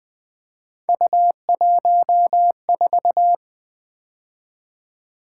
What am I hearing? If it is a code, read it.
Morse code, heard as a keyed tone, about 20 wpm, 700 Hz. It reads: U14
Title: Three Letter-Number Combinations - 20wpm